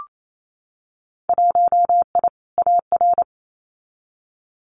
Morse code, heard as a keyed tone, about 28 wpm, 700 Hz. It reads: 1I AR
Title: Winter Field Day Exchanges Encore 28wpm